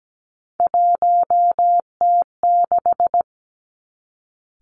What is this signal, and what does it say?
Morse code, heard as a keyed tone, about 17 wpm, 700 Hz. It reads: 1T6
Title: Three Challenging Letter-Number Combinations 17wpm